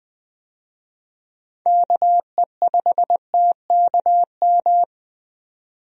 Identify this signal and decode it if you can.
Morse code, heard as a keyed tone, about 20 wpm, 700 Hz. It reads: KE5TKM
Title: USA Call Signs - 6 Characters - 20wpm